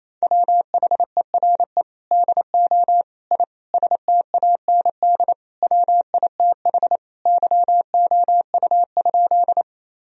Morse code, heard as a keyed tone, about 28 wpm, 700 Hz. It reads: WHERE DO I STAND WITH YOU?
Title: Sentences from Top 1000 Words - Review Part 1 - 28wpm